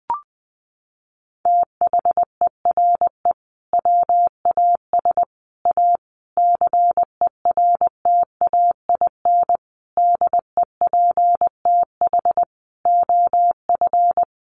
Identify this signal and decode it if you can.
Morse code, heard as a keyed tone, about 20 wpm, 700 Hz. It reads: THERE WAS A CERTAIN DEPTH OF
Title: Sentences from Top 1200 Words 20wpm